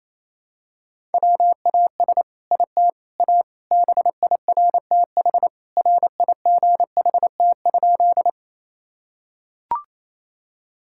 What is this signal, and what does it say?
Morse code, heard as a keyed tone, about 28 wpm, 700 Hz. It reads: WAS IT A BIRTH RIGHT?
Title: Sentences from Top 1000 Words - Review Part 5 - 28wpm